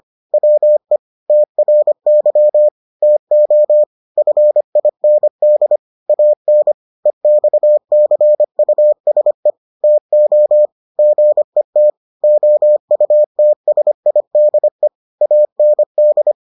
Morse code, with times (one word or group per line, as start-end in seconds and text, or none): 0.3-1.0 WE
1.3-2.7 TRY
3.0-3.8 TO
4.2-5.8 FIND
6.1-6.7 AN
7.1-9.5 EXCUSE
9.8-10.7 TO
11.0-11.9 GET
12.2-14.9 OUTSIDE
15.2-16.3 AND